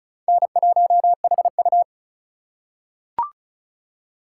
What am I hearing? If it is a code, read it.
Morse code, heard as a keyed tone, about 34 wpm, 690 Hz. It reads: N1HU
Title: USA Call Signs - 4x Character Spacing - 35wpm